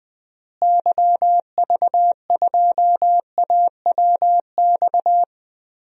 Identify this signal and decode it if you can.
Morse code, heard as a keyed tone, about 21 wpm, 700 Hz. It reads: YV2AWX
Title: DXCC-100 Call Signs - 6 Characters - Repeated 2x - 20wpm